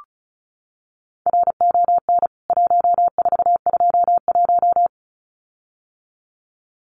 Morse code, heard as a keyed tone, about 34 wpm, 700 Hz. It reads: RON 1421
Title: CWT Exchange 35wpm